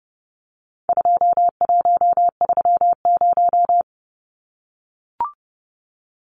Morse code, heard as a keyed tone, about 30 wpm, 700 Hz. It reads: 2130
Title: Numbers - 4-Digit 30wpm